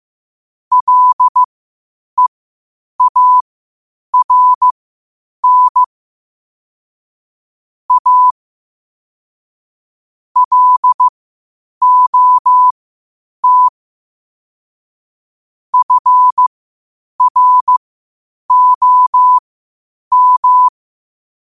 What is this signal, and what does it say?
Morse code, heard as a keyed tone, about 15 wpm, spaced as for 8 wpm, 1000 Hz. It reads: LEARN A LOT FROM
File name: qotdfiles / 8wpm